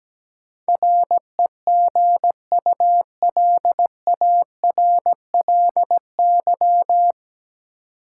Morse code, heard as a keyed tone, about 17 wpm, 700 Hz. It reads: REGULARLY
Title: Sentences from Top 1400 Words - New Word Drill - 17wpm